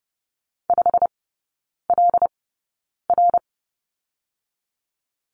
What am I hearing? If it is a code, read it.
Morse code, heard as a keyed tone, about 30 wpm, spaced as for 9 wpm, 700 Hz. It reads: 5LR
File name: Lesson 16e - Letter c - Three Characters 30wpm